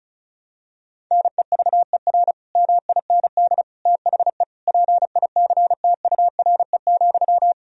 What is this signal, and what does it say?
Morse code, heard as a keyed tone, about 34 wpm, 700 Hz. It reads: NEVER MIND THE PICTURE,
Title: Sentences from Top 1000 Words - Review Part 1 - 35wpm